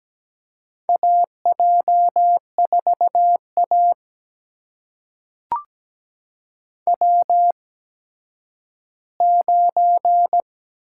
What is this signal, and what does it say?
Morse code, heard as a keyed tone, about 17 wpm, spaced as for 10 wpm, 700 Hz. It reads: AJ4A W9
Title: USA Call Signs - 4 Characters - 8x Character Spacing - 17wpm